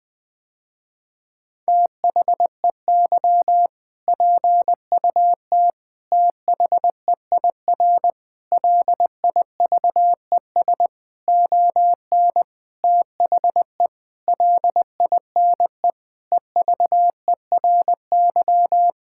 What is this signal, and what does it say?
Morse code, heard as a keyed tone, about 21 wpm, 700 Hz. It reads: THEY PUT THEIR LIVES ON THE LINE EVERY
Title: Sentences with 8-Word Phrases - Spoken Beginning - 20wpm